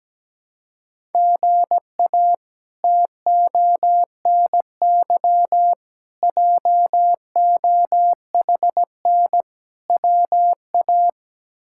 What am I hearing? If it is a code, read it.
Morse code, heard as a keyed tone, about 17 wpm, 700 Hz. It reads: GA TONY JOHN WA
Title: K1USN SST QSO Practice - 17wpm